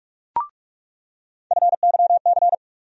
Morse code, heard as a keyed tone, about 46 wpm, 700 Hz. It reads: FYC
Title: USA Call Sign Postfixes - 3 Letters 45wpm